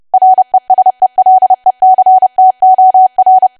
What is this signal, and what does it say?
Morse code, heard as a keyed tone, about 30 wpm, 750 Hz. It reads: RESELECTOR
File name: cwa / W1AW-Level3 / 30WPM